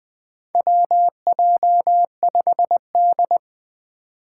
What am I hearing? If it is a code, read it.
Morse code, heard as a keyed tone, about 20 wpm, 700 Hz. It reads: WJ5D
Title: USA Call Signs - 4 Characters - 3x Character Spacing - 20wpm